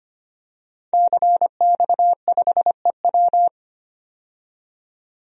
Morse code, heard as a keyed tone, about 25 wpm, 700 Hz. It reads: CX5EW
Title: DXCC-100 Call Signs - 5 Characters - 25wpm